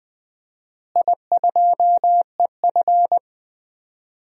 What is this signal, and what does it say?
Morse code, heard as a keyed tone, about 20 wpm, 700 Hz. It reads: I2EF
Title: DXCC-100 Call Signs - 4 Characters - Repeated 3x - 20wpm